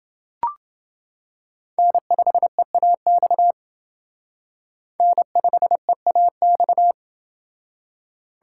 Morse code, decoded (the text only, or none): N5EAX N5EAX